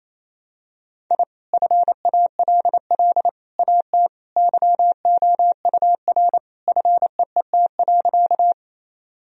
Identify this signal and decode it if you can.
Morse code, heard as a keyed tone, about 28 wpm, 700 Hz. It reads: I FALL AT YOUR FEET.
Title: Sentences from Top 1000 Words - Review Part 2 - 28wpm